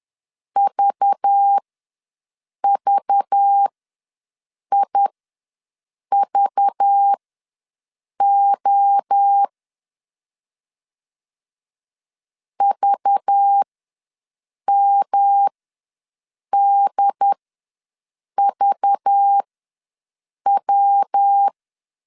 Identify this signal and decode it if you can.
Morse code, heard as a keyed tone, about 10 wpm, spaced as for 6 wpm, 800 Hz. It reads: VVIVO VMDVW
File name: LEZIONI / 5 LBVC / 5 LBV ws 10wpm 3645